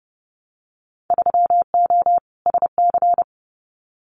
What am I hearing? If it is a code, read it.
Morse code, heard as a keyed tone, about 30 wpm, 700 Hz. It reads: 3O SC
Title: Winter Field Day Exchanges Encore 30wpm